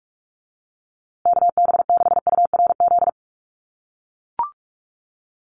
Morse code, heard as a keyed tone, about 44 wpm, 690 Hz. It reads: KB6URZ
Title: USA Call Signs Repeated 3x 45wpm